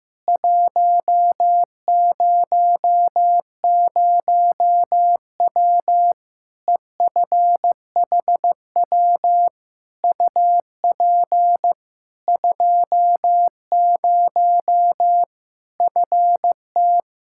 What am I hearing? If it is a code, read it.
Morse code, heard as a keyed tone, about 15 wpm, 700 Hz. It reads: 100W EFHW UP 20 FT
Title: K1USN SST QSO Practice 15wpm